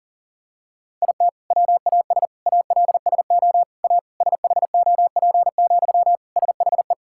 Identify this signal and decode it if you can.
Morse code, heard as a keyed tone, about 40 wpm, 700 Hz. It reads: IT WAS ALSO A SHOP, SHE
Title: Sentences from Top 1600 Words 40wpm